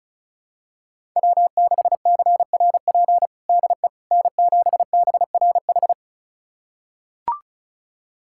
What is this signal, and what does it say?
Morse code, heard as a keyed tone, about 34 wpm, 700 Hz. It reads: W6CRP DE N7BRH
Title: USA Callsign DE Callsign - 5 Characters - 35wpm